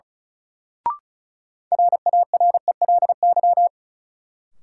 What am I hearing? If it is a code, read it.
Morse code, heard as a keyed tone, about 36 wpm, 700 Hz. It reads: RARELY
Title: Sentences from Top 1400 Words - New Word Drill - 35wpm